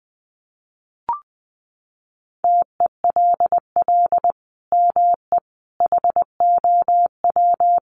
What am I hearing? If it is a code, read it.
Morse code, heard as a keyed tone, about 20 wpm, 700 Hz. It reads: TELL ME HOW
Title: Sentences from Top 1500 Words 20wpm